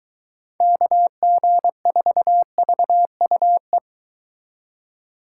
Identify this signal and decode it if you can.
Morse code, heard as a keyed tone, about 22 wpm, 700 Hz. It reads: KG4VUE